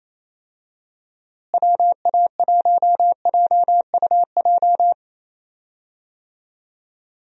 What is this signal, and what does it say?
Morse code, heard as a keyed tone, about 27 wpm, 700 Hz. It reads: WA1JUJ
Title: USA Call Signs - 6 Characters - Repeated 3x - 28wpm